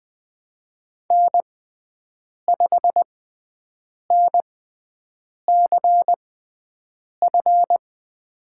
Morse code, heard as a keyed tone, about 20 wpm, spaced as for 7 wpm, 700 Hz. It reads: N5NCF
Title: USA Call Signs - 5 Characters - 6x Character Spacing - 20wpm